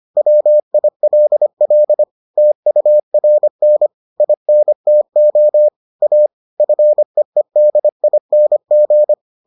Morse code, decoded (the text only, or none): WILL TURN INTO A FEEDING